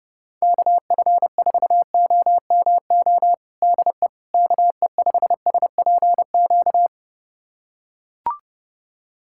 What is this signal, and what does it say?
Morse code, heard as a keyed tone, about 32 wpm, 700 Hz. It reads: KF4OMO DE KE5SPQ
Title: USA Callsign DE Callsign - 6 Characters - 30wpm